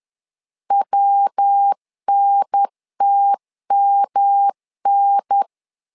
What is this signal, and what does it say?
Morse code, heard as a keyed tone, about 10 wpm, 800 Hz. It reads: WNTMN
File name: LEZIONI / 3 MWRO / 3 MW 10wpm 1959